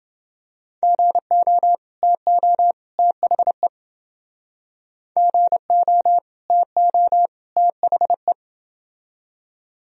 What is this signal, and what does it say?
Morse code, heard as a keyed tone, about 29 wpm, 700 Hz. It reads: GO TO THE GO TO THE